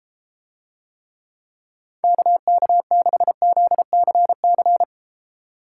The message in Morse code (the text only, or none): KK6ZCC